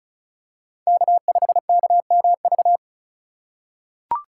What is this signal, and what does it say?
Morse code, heard as a keyed tone, about 34 wpm, 700 Hz. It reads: K5KMV
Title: USA Call Signs - 5 Characters - 4x Character Spacing - 35wpm